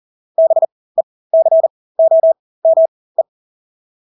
Morse code, heard as a keyed tone, about 40 wpm, spaced as for 20 wpm, 650 Hz. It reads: BECOME
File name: Engelske ord 40-20 wpm